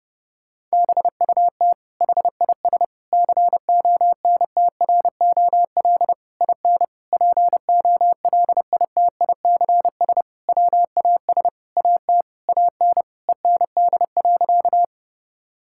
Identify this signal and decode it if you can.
Morse code, heard as a keyed tone, about 30 wpm, 700 Hz. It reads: BUT HIS CONTROL IN POLITICS WAS AT AN END.
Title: Sentences from Top 1000 Words - Review Part 6 - 30wpm